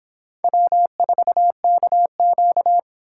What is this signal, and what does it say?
Morse code, heard as a keyed tone, about 29 wpm, 700 Hz. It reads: W4KQ